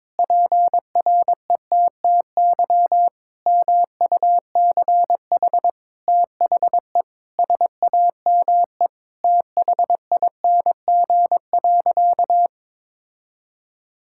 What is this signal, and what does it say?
Morse code, heard as a keyed tone, about 22 wpm, 700 Hz. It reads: PRETTY MUCH THE SAME THING.
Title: Sentences from Top 1600 Words 22wpm